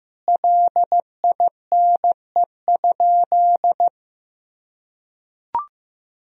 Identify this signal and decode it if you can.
Morse code, heard as a keyed tone, about 15 wpm, 700 Hz. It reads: LINE?
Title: Sentences from Top 1000 Words - Review Part 1 - 15wpm